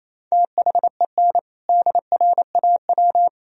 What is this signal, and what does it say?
Morse code, heard as a keyed tone, about 28 wpm, 700 Hz. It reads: THEN DRAW
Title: Sentences from Top 1000 Words - Review Part 1 - 28wpm